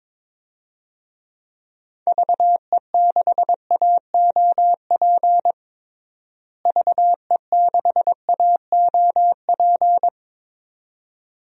Morse code, heard as a keyed tone, about 22 wpm, 700 Hz. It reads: VE6AOP VE6AOP
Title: DXCC-100 Call Signs - 6 Characters - Repeated 2x - 22wpm